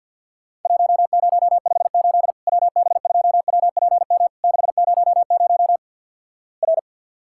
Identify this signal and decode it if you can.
Morse code, heard as a keyed tone, about 50 wpm, 700 Hz. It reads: 1058 WB2WPM 600 R
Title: Worked All Europe DX Contest - QTC Exchange - 50wpm